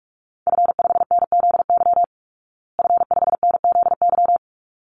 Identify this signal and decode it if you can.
Morse code, heard as a keyed tone, about 44 wpm, 700 Hz. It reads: F5NZY F5NZY
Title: Call Signs Repeated 3x - 45wpm